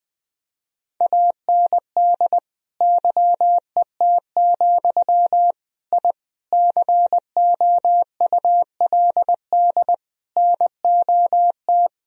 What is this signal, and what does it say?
Morse code, heard as a keyed tone, about 20 wpm, 700 Hz. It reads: AND YET, I COULD NOT
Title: Sentences from Top 1000 Words - Review Part 1 - 20wpm